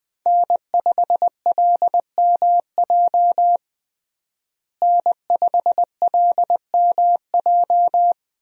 Morse code, heard as a keyed tone, about 20 wpm, 700 Hz. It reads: N5LMJ N5LMJ
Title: USA Call Signs - 5 Characters - Repeated 3x - 20wpm